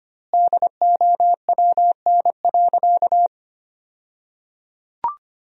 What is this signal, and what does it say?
Morse code, heard as a keyed tone, about 25 wpm, 700 Hz. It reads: DOWN.
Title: 2x Word-Spacing - Sentences from Top 500 Words - 25wpm